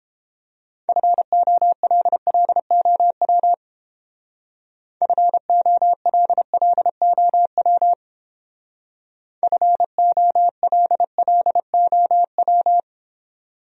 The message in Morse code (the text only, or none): FOLLOW FOLLOW FOLLOW